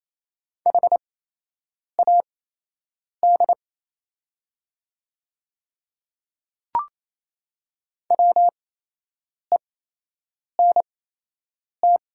Morse code, heard as a keyed tone, about 28 wpm, spaced as for 8 wpm, 700 Hz. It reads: HAD WENT
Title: 8x Character Spacing - Top 200 Words - 28wpm